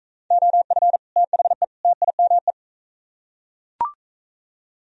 Morse code, heard as a keyed tone, about 40 wpm, 700 Hz. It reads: OF THE TIME